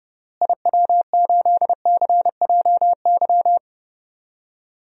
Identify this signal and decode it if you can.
Morse code, heard as a keyed tone, about 30 wpm, 700 Hz. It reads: IW8CJY
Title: DXCC-100 Call Signs - 6 Characters - Repeated 2x - 30wpm